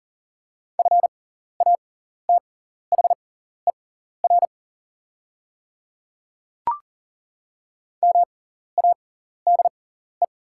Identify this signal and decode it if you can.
Morse code, heard as a keyed tone, about 39 wpm, spaced as for 14 wpm, 700 Hz. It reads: FATHER MADE
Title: 6x Character Spacing - Top 200 Words - 40wpm